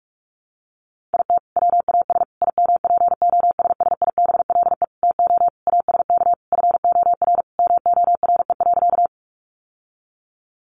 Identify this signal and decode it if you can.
Morse code, heard as a keyed tone, about 44 wpm, 700 Hz. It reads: IT WAS IMPOSSIBLE TO ASK FOR MORE.
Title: Sentences from Top 1100 Words 45wpm